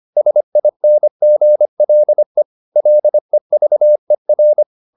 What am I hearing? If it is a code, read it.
Morse code, heard as a keyed tone, about 25 wpm, 600 Hz. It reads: SINGLE LEVER